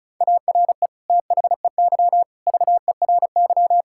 Morse code, heard as a keyed tone, about 36 wpm, 700 Hz. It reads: ARE THEY VERY
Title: Sentences from Top 1000 Words - Review Part 5 - 35wpm